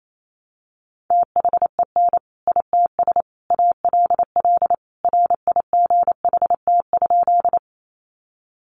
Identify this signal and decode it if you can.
Morse code, heard as a keyed tone, about 28 wpm, 700 Hz. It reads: THEN ITS ALL RIGHT?
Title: Sentences from Top 600 Words 28wpm